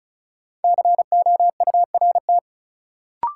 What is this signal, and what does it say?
Morse code, heard as a keyed tone, about 36 wpm, 700 Hz. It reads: COURT
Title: Sentences from Top 1600 Words - New Word Drill Alt - 35wpm